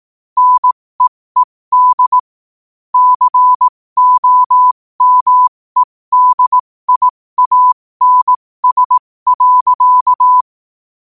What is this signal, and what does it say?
Morse code, heard as a keyed tone, about 18 wpm, spaced as for 15 wpm, 1000 Hz. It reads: NEED COMEDIANS.
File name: qotdfiles / 15wpm